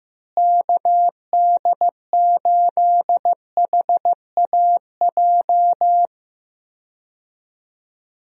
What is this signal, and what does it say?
Morse code, heard as a keyed tone, about 15 wpm, 700 Hz. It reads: KD8HAJ
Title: USA Call Signs - 6 Characters - Repeated 3x - 15wpm